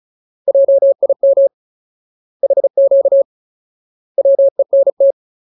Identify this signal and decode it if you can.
Morse code, heard as a keyed tone, about 34 wpm, 550 Hz. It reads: JIM HQ WENT
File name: Ultimate Word List / 35WPM Ultimate